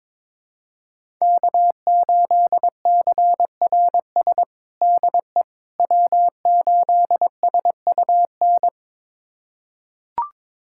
Morse code, heard as a keyed tone, about 22 wpm, 700 Hz. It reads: K8CRS DE W8SUN
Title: USA Callsign DE Callsign - 5 Characters - 22wpm